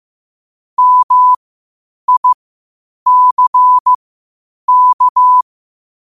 Morse code, heard as a keyed tone, about 15 wpm, spaced as for 8 wpm, 1000 Hz. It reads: MICK
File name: qotdfiles / 8wpm